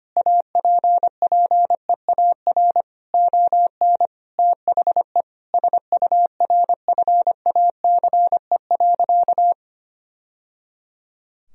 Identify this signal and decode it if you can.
Morse code, heard as a keyed tone, about 25 wpm, 700 Hz. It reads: APPEAR ON THE SURFACE.
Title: Sentences from Top 1000 Words - Review Part 3 - 25wpm